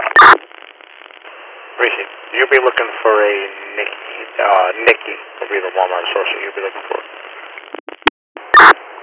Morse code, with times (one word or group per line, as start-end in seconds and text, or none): none